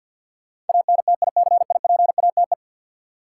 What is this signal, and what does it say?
Morse code, heard as a keyed tone, about 53 wpm, 700 Hz. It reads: ANTICIPATE
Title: Sentences from Top 1500 Words - New Word Drill - 50wpm